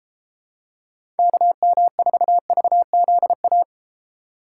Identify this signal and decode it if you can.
Morse code, heard as a keyed tone, about 32 wpm, 700 Hz. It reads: KM4VZA